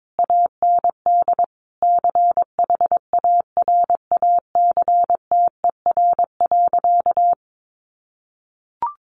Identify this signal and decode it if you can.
Morse code, heard as a keyed tone, about 22 wpm, 700 Hz. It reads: AND CHARACTER.
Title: Sentences from Top 1400 Words 22wpm